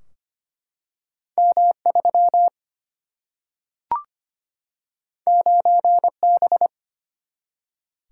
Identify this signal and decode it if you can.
Morse code, heard as a keyed tone, about 25 wpm, 700 Hz. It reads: M 3 9 B
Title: Single-Letter Single-Number 25wpm